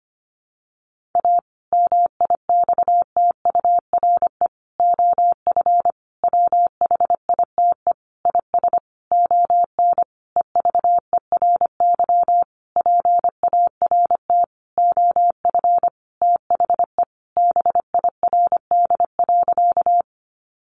Morse code, heard as a keyed tone, about 25 wpm, 700 Hz. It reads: A MIXTURE OF WHITE IS ON EVERY PART OF THE BIRD.
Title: Sentences from Top 1500 Words 25wpm